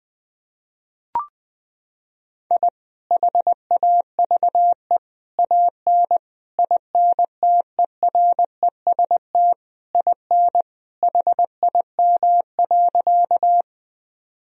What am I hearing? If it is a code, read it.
Morse code, heard as a keyed tone, about 20 wpm, 700 Hz. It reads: I HAVE AN INTEREST IN HIM.
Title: Sentences from Top 1000 Words - Review Part 2 - 20wpm